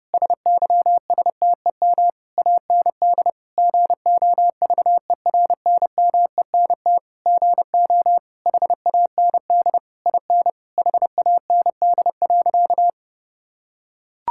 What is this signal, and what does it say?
Morse code, heard as a keyed tone, about 30 wpm, 700 Hz. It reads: SYSTEM AND GOVERNMENT GO HAND IN HAND.
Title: Sentences from Top 1000 Words - Review Part 5 - 30wpm